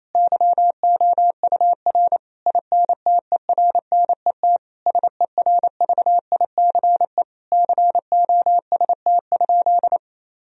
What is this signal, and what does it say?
Morse code, heard as a keyed tone, about 28 wpm, 700 Hz. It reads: YOUR INTERNET SERVICE COST?
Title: Sentences from Top 1000 Words - Review Part 4 - 28wpm